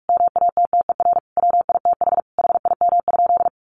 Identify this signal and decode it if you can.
Morse code, heard as a keyed tone, about 44 wpm, 700 Hz. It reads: MATTER WITH HIM?
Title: Sentences from Top 700 Words 45wpm